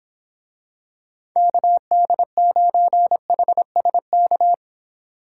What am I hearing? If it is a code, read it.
Morse code, heard as a keyed tone, about 26 wpm, 700 Hz. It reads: KD9HSK